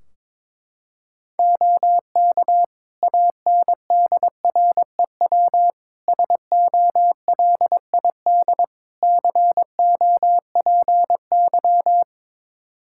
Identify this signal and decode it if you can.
Morse code, heard as a keyed tone, about 22 wpm, 700 Hz. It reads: OK ANDREW SOLID COPY
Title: Practice QSO - Exchange 3 - More Info - 22wpm